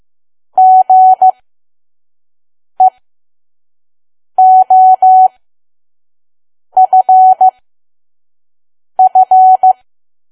Morse code, heard as a keyed tone, about 15 wpm, spaced as for 5 wpm, 750 Hz. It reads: GEOFF
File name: Archive / 5 WPM files / PM